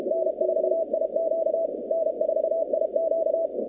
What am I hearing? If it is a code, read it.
Morse code, heard as a keyed tone, about 32 wpm, 650 Hz, with noise. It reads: N4IQ N4IQ
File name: 2019-cw-wpx-kt5j-k5pi